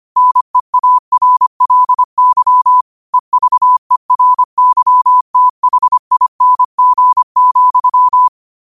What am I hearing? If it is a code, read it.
Morse code, heard as a keyed tone, about 25 wpm, 1000 Hz. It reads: NEARLY EVERYTHING,
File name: qotdfiles / 25wpm